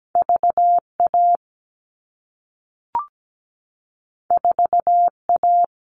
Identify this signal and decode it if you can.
Morse code, heard as a keyed tone, about 17 wpm, 700 Hz. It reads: VA 4A
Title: All Cut Numbers - 2-Digit 17wpm